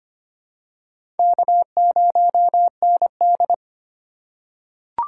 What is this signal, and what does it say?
Morse code, heard as a keyed tone, about 25 wpm, 700 Hz. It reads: K0ND
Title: USA Call Signs - 4 Characters - 3x Character Spacing - 25wpm